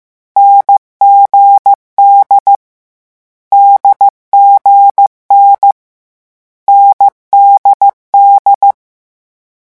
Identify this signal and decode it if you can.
Morse code, heard as a keyed tone, about 15 wpm, 800 Hz. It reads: NGD DGN NDD
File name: morse mp3 / lecon4-vitesse 14